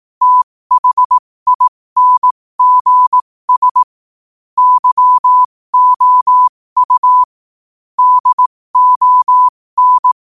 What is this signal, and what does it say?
Morse code, heard as a keyed tone, about 18 wpm, spaced as for 15 wpm, 1000 Hz. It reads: THINGS YOU DON
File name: qotdfiles / 15wpm